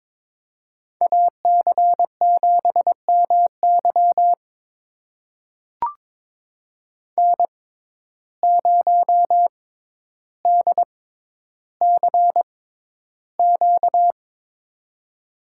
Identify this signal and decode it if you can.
Morse code, heard as a keyed tone, about 21 wpm, spaced as for 11 wpm, 700 Hz. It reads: AC7MY N0DCQ
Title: USA Call Signs - 5 Characters - 6x Character Spacing - 22wpm